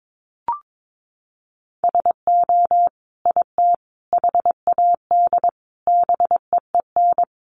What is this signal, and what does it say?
Morse code, heard as a keyed tone, about 22 wpm, 700 Hz. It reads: SO IT HAD BEEN
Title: Sentences from Top 1200 Words 22wpm